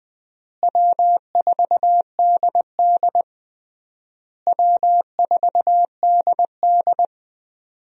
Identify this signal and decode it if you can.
Morse code, heard as a keyed tone, about 20 wpm, 700 Hz. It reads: W4DD W4DD
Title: USA Call Signs - 4 Characters - Repeated 3x - 20wpm